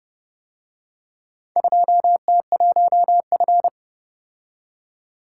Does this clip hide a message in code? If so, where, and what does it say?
Yes: Morse code from 1.6 to 3.7 s, heard as a keyed tone, about 28 wpm, 700 Hz. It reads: 2T1F